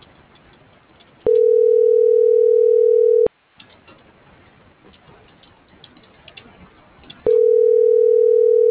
Ringback tone (North America), 440 + 480 Hz.